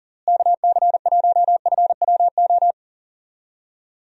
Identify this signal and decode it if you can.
Morse code, heard as a keyed tone, about 40 wpm, 700 Hz. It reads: KC1FWO
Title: USA Call Signs - 6 Characters - Repeated 3x - 40wpm